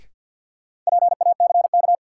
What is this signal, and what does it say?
Morse code, heard as a keyed tone, about 52 wpm, 700 Hz. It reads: PACK
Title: Sentences from Top 1300 Words - New Word Drill Alt - 50wpm